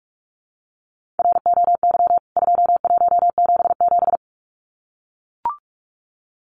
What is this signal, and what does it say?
Morse code, heard as a keyed tone, about 44 wpm, 700 Hz. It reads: ROY 2177
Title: CWT Exchange 45wpm